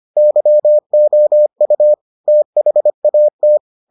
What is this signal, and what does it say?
Morse code, heard as a keyed tone, about 25 wpm, 600 Hz. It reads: YOU THAT